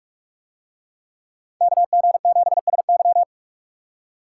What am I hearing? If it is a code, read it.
Morse code, heard as a keyed tone, about 46 wpm, 700 Hz. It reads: KG7SY